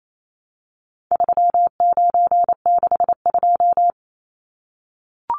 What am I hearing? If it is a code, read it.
Morse code, heard as a keyed tone, about 28 wpm, 700 Hz. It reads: 3962
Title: Numbers - 4-Digit 28wpm